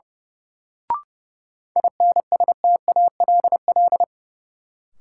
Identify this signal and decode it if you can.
Morse code, heard as a keyed tone, about 30 wpm, 700 Hz. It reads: INSTALL
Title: Sentences from Top 1600 Words - New Word Drill Alt - 30wpm